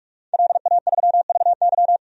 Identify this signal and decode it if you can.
Morse code, heard as a keyed tone, about 46 wpm, 700 Hz. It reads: LA3VY
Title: DXCC-100 Call Signs - 5 Characters - 45wpm